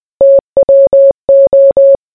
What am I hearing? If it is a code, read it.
Morse code, heard as a keyed tone, about 20 wpm, 550 Hz. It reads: TWO